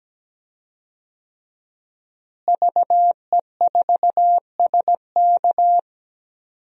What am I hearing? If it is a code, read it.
Morse code, heard as a keyed tone, about 17 wpm, 700 Hz. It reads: VE4SK
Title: DXCC-100 Call Signs - 5 Characters - Repeated 2x - 17wpm